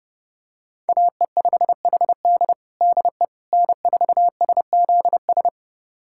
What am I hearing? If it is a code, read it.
Morse code, heard as a keyed tone, about 32 wpm, 700 Hz. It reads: AE5HD DE N4SZS
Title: USA Callsign DE Callsign - 5 Characters - 30wpm